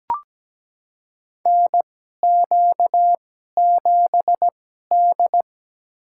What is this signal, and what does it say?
Morse code, heard as a keyed tone, about 17 wpm, spaced as for 12 wpm, 700 Hz. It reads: NQ7D
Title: USA Call Signs - 4 Characters - 2x Character Spacing - 17wpm